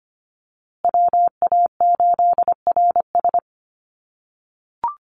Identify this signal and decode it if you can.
Morse code, heard as a keyed tone, about 25 wpm, 700 Hz. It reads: WA8RS
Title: USA Call Signs Repeated 3x 25wpm